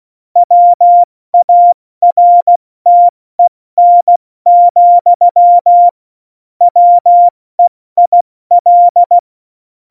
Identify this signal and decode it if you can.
Morse code, heard as a keyed tone, about 16 wpm, spaced as for 14 wpm, 700 Hz. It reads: WARTEN, WEIL
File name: Morsekurs / Witze / Witz 16. Juli 2019, 70 F80